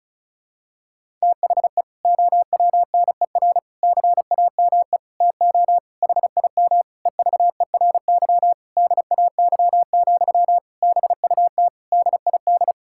Morse code, heard as a keyed tone, about 36 wpm, 700 Hz. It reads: THE OWNER CAME TO HIM EVERY DAY, BUT DID
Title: Sentences from Top 1600 Words 35wpm